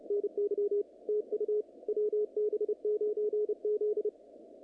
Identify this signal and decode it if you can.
Morse code, heard as a keyed tone, about 30 wpm, 440 Hz, with radio noise. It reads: NY TU WB9Z
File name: contests / naqp-cw-k5tr-op-k5pi- -2200